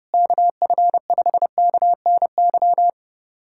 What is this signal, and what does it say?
Morse code, heard as a keyed tone, about 31 wpm, 700 Hz. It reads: KF5KNY
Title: USA Callsign DE Callsign - 6 Characters - 30wpm